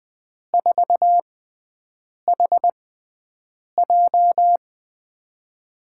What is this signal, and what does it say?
Morse code, heard as a keyed tone, about 20 wpm, spaced as for 7 wpm, 700 Hz. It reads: 4HJ
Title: USA Call Signs - 5 Characters - 6x Character Spacing - 20wpm